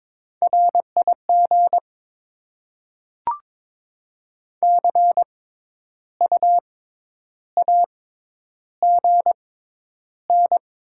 Morse code, heard as a keyed tone, about 22 wpm, spaced as for 10 wpm, 700 Hz. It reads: RIG CUAGN
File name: Top 20 QSO Elements - 6x Character Spacing - 22wpm